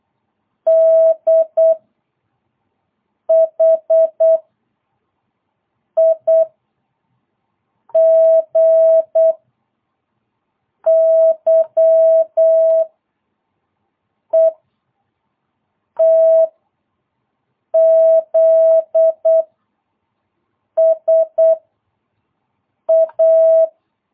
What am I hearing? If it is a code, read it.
Morse code, heard as a keyed tone, about 8 wpm, 660 Hz. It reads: D H I G Y E T Z S A
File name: jh8cbh / 2022cw